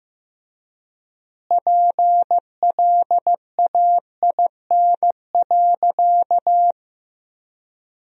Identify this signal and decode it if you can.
Morse code, heard as a keyed tone, about 15 wpm, 700 Hz. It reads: PLAIN.
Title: 3x Word-Spacing - Sentences from Top 400 Words - 15wpm